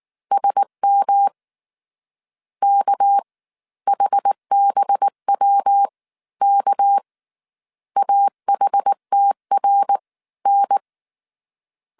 Morse code, heard as a keyed tone, about 19 wpm, 800 Hz. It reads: SM K HBW K AHTR N